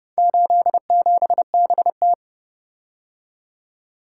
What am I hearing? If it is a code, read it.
Morse code, heard as a keyed tone, about 29 wpm, 700 Hz. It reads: 87BT